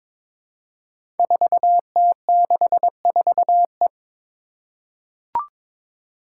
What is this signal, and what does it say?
Morse code, heard as a keyed tone, about 22 wpm, 700 Hz. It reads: 4T64E
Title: Five Challenging Letter-Number Combinations 22wpm